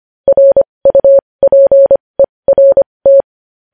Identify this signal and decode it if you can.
Morse code, heard as a keyed tone, about 25 wpm, spaced as for 20 wpm, 550 Hz. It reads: RUPERT
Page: news-headlines-in-morse-code-at-20-wpm